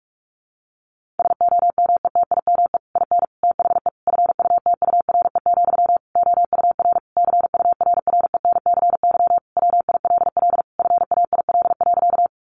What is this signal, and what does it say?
Morse code, heard as a keyed tone, about 44 wpm, 700 Hz. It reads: SOMETIME IN THE FUTURE, OUR CURRENCY WILL FAIL.
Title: Sentences from Top 1500 Words 45wpm